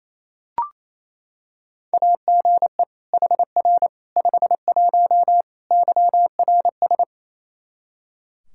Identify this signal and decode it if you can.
Morse code, heard as a keyed tone, about 28 wpm, 700 Hz. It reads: AGE HR 51 YRS
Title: Practice QSO - Exchange 3 - More Info - 28wpm